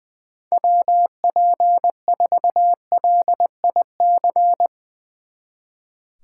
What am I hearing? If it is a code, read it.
Morse code, heard as a keyed tone, about 21 wpm, 700 Hz. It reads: WP4LIC
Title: Speed Racing - USA 2x3 Call Signs 20wpm